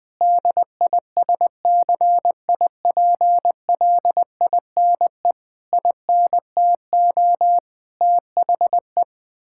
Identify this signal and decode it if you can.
Morse code, heard as a keyed tone, about 20 wpm, 700 Hz. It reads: DISCIPLINE INTO THE
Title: Sentences from Top 1000 Words - Review Part 6 - 20wpm